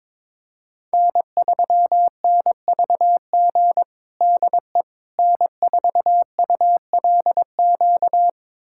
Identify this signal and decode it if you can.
Morse code, heard as a keyed tone, about 22 wpm, 700 Hz. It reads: N3NVG DE N4ULQ
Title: USA Callsign DE Callsign - 5 Characters - 22wpm